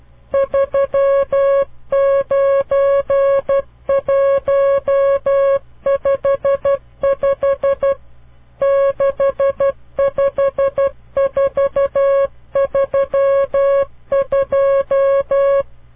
Morse code, heard as a keyed tone, about 12 wpm, 550 Hz. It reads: 39155 65432
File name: CW Learning Programs / farnsworth / unit 5